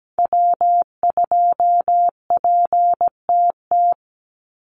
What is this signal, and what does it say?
Morse code, heard as a keyed tone, about 17 wpm, 700 Hz. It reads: W2PTT
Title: Call Signs - 17wpm